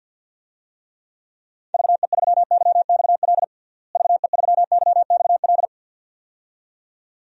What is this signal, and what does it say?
Morse code, heard as a keyed tone, about 51 wpm, 700 Hz. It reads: VE3YXL VE3YXL
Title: DXCC-100 Call Signs - 6 Characters - Repeated 2x - 50wpm